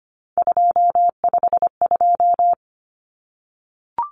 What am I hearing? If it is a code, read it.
Morse code, heard as a keyed tone, about 25 wpm, 700 Hz. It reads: 252
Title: Common Cut Numbers - 3-Digit 25wpm